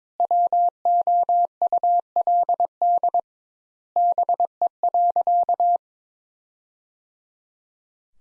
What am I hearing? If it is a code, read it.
Morse code, heard as a keyed tone, about 22 wpm, 700 Hz. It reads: WOULD BE.
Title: Sentences Top 200 Words - 22wpm